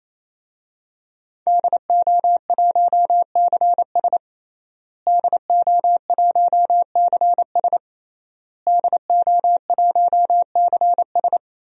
Morse code, heard as a keyed tone, about 28 wpm, 700 Hz. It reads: DO1CS DO1CS DO1CS
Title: DXCC-100 Call Signs - 5 Characters - Repeated 3x - 28wpm